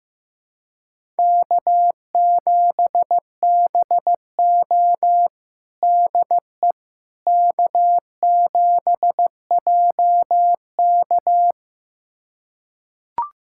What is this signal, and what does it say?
Morse code, heard as a keyed tone, about 15 wpm, 700 Hz. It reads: K7BO DE K7JK
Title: USA Callsign DE Callsign - 4 Characters - 15wpm